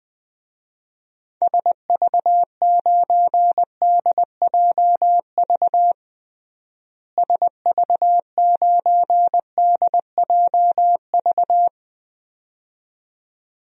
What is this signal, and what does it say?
Morse code, heard as a keyed tone, about 20 wpm, 700 Hz. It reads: SV9DJV SV9DJV
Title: DXCC-100 Call Signs - 6 Characters - Repeated 2x - 20wpm